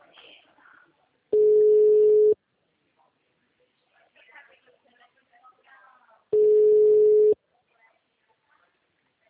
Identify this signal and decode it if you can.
Ringback tone (Europe, 425 Hz), with speech.
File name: out- -2003- - - .72